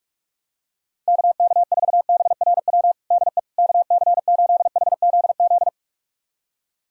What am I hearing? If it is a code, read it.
Morse code, heard as a keyed tone, about 47 wpm, 700 Hz. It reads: KK4BRW DE KC8HZZ
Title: USA Callsign DE Callsign - 6 Characters - 45wpm